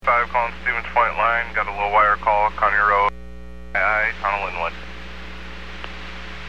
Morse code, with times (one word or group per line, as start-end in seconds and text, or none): none